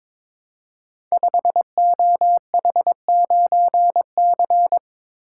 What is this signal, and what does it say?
Morse code, heard as a keyed tone, about 22 wpm, 700 Hz. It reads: 5OH9C
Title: Code Groups - 5 Letter:Numbers - 22wpm